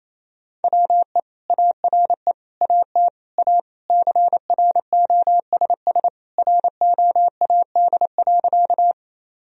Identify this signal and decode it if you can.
Morse code, heard as a keyed tone, about 27 wpm, 700 Hz. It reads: WE ARE AT A CROSS ROAD.
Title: Sentences from Top 1000 Words - Review Part 2 - 28wpm